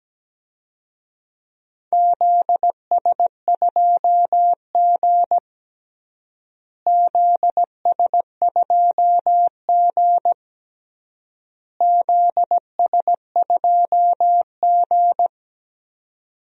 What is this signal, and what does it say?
Morse code, heard as a keyed tone, about 17 wpm, 700 Hz. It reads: ZS2G ZS2G ZS2G
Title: DXCC-100 Call Signs - 4 Characters - Repeated 3x - 17wpm